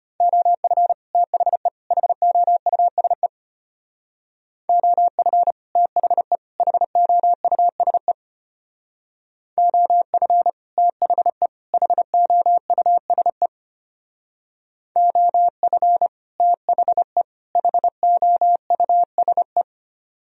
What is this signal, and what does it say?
Morse code, heard as a keyed tone, about 30 wpm, 700 Hz. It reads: OF THE HOUSE OF THE HOUSE OF THE HOUSE OF THE HOUSE
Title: Speed-Racing - Sets of 3 Words 25wpm